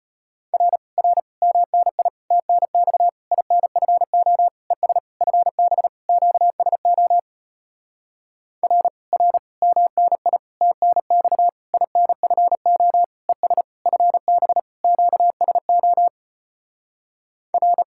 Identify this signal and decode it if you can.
Morse code, heard as a keyed tone, about 33 wpm, 700 Hz. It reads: R R MNI TNX INFO ES FB QSO R R MNI TNX INFO ES FB QSO R